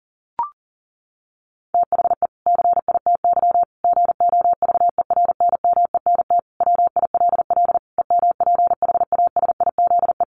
Morse code, heard as a keyed tone, about 40 wpm, 700 Hz. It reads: THE CITY GOVERNMENT WILL EMPHASIZE
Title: Sentences from Top 1500 Words 40wpm